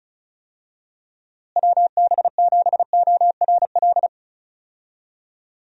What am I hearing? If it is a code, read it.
Morse code, heard as a keyed tone, about 34 wpm, 700 Hz. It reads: WB7ORL
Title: USA Call Signs - 6 Characters - Repeated 3x - 35wpm